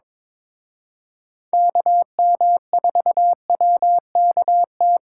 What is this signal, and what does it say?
Morse code, heard as a keyed tone, about 25 wpm, 700 Hz. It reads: KM4WKT
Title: Speed Racing - USA Call Signs 22wpm